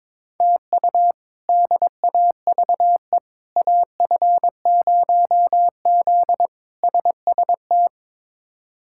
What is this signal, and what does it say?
Morse code, heard as a keyed tone, about 22 wpm, 700 Hz. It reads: TU DAVE AF0Z SST
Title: K1USN SST QSO Practice - 22wpm